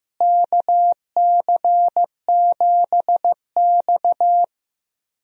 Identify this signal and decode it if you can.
Morse code, heard as a keyed tone, about 15 wpm, 700 Hz. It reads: KC7X
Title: USA Call Signs - 4 Characters - Repeated 3x - 15wpm